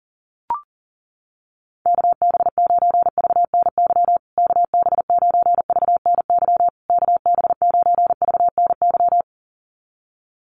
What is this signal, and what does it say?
Morse code, heard as a keyed tone, about 37 wpm, 700 Hz. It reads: KB9VNY KB9VNY KB9VNY
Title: USA Call Signs Repeated 3x 40wpm